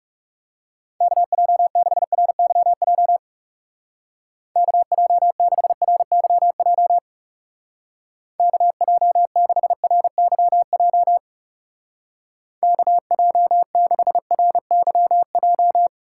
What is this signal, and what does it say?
Morse code, heard as a keyed tone, about 36 wpm, 700 Hz. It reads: KJ6RYJ KJ6RYJ KJ6RYJ KJ6RYJ
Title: Speed Racing - USA 2x3 Call Signs 30wpm